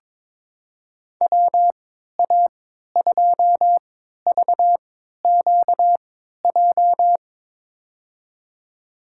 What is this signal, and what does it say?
Morse code, heard as a keyed tone, about 22 wpm, spaced as for 12 wpm, 700 Hz. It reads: WA2VQJ
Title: USA Call Signs - 6 Characters - 3x Character Spacing - 22wpm